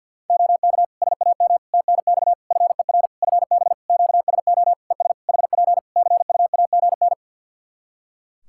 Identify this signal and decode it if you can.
Morse code, heard as a keyed tone, about 51 wpm, 700 Hz. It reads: OK SAM TNX FER FB QSO ES HP CUAGN
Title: Practice QSO - Exchange 4 - Ending - 50wpm